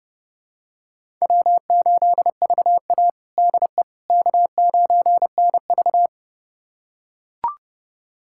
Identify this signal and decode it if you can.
Morse code, heard as a keyed tone, about 30 wpm, 700 Hz. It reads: W8VA DE K9NV
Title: USA Callsign DE Callsign - 4 Characters - 30wpm